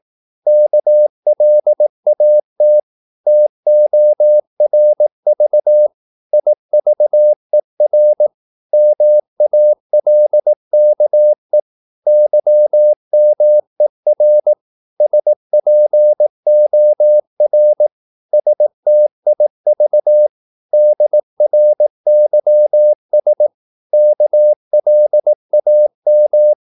Morse code, heard as a keyed tone, about 18 wpm, 600 Hz. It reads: KLAT TORV IVER MALKE YMER SPOR STIV DRYS KLAM
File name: Korte ord DK 18wpm